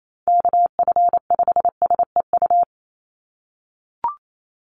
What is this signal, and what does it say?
Morse code, heard as a keyed tone, about 28 wpm, 700 Hz. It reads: KF5SEU
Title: USA Call Signs 28wpm